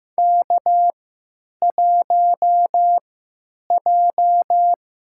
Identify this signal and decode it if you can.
Morse code, heard as a keyed tone, about 16 wpm, spaced as for 9 wpm, 700 Hz. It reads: K1J
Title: USA Call Signs - 3x Character Spacing - 15wpm